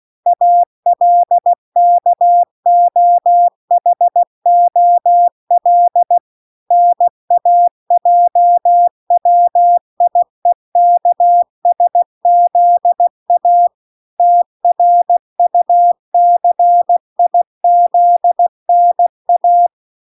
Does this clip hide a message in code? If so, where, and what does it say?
Yes: Morse code from 0.3 to 19.7 s, heard as a keyed tone, about 16 wpm, 700 Hz. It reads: ALKOHOL NAJWIEKSZA TRUCIZNA